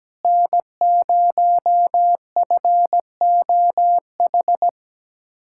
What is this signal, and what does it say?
Morse code, heard as a keyed tone, about 17 wpm, 700 Hz. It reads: N0FOH
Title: USA Callsign DE Callsign - 5 Characters - 17wpm